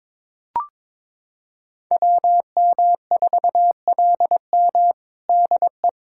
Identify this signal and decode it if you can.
Morse code, heard as a keyed tone, about 22 wpm, 700 Hz. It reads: WM4LM DE
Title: USA Callsign DE Callsign - 5 Characters - 22wpm